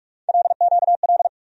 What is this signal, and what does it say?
Morse code, heard as a keyed tone, about 46 wpm, 700 Hz. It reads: LQL
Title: USA Call Sign Postfixes - 3 Letters 45wpm